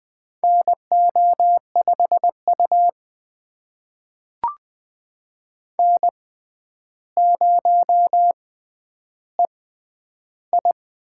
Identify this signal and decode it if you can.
Morse code, heard as a keyed tone, about 20 wpm, spaced as for 10 wpm, 700 Hz. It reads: NO5U N0EI
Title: USA Call Signs - 4 Characters - 6x Character Spacing - 20wpm